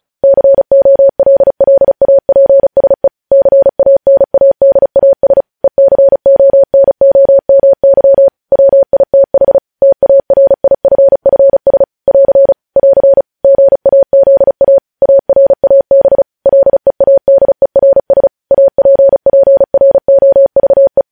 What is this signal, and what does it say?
Morse code, heard as a keyed tone, about 36 wpm, 550 Hz. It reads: COLLAPSE CANADAS ECONOMY WITH TARIFFS <AR> <AR> GAZA ARAB LEADERS APPROVE
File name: podcast / News headlines-1x-35x35